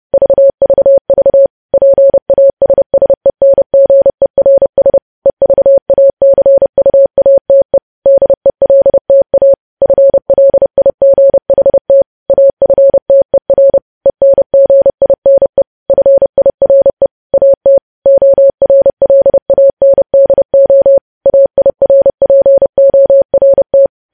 Morse code, heard as a keyed tone, about 30 wpm, 550 Hz. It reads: VVV PASSENGERS EVACUATE DELTA FLIGHT AFTER ENGINE FIRE AT ORLANDO AIRPORT
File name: podcast / News headlines-1x-30x30